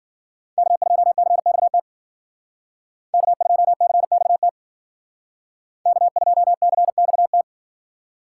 Morse code, heard as a keyed tone, about 60 wpm, 700 Hz. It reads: K2CXT K2CXT K2CXT